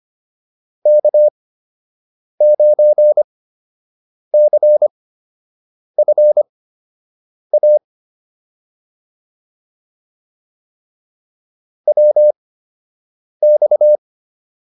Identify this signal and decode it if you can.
Morse code, heard as a keyed tone, about 25 wpm, spaced as for 7 wpm, 600 Hz. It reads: K9CFA WX